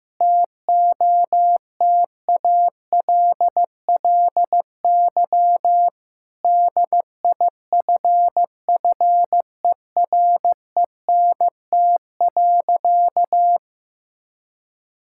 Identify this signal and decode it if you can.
Morse code, heard as a keyed tone, about 15 wpm, 700 Hz. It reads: TOTALLY DIFFERENT.
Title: Sentences from Top 1600 Words 15wpm